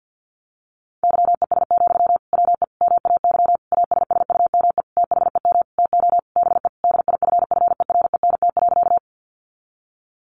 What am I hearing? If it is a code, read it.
Morse code, heard as a keyed tone, about 50 wpm, 700 Hz. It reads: YES, WE MAY ASSUME THEM TO BE DIFFERENT.
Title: Sentences from Top 1500 Words 50wpm